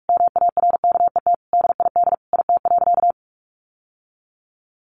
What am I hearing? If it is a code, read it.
Morse code, heard as a keyed tone, about 44 wpm, 700 Hz. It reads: MARKET DID IT.
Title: Sentences from Top 600 Words 45wpm